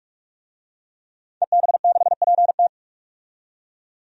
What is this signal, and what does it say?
Morse code, heard as a keyed tone, about 41 wpm, 700 Hz. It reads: EB6PT
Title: DXCC-100 Call Signs - 5 Characters - 45wpm